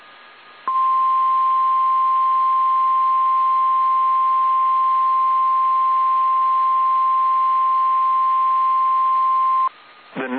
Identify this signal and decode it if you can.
Emergency Alert System (SAME) tones (the 1050 Hz attention tone). It is heard from 0.6 to 9.7 s.
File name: eas / KSDZKDJL / KSDZKDJL